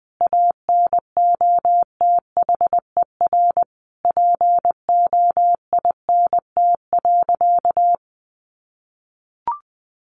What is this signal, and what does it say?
Morse code, heard as a keyed tone, about 20 wpm, 700 Hz. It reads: ANOTHER POINT.
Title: Sentences from Top 600 Words 20wpm